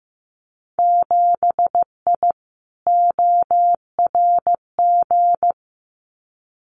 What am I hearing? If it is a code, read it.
Morse code, heard as a keyed tone, about 15 wpm, 700 Hz. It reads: 7I ORG
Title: Winter Field Day Exchanges Encore 15wpm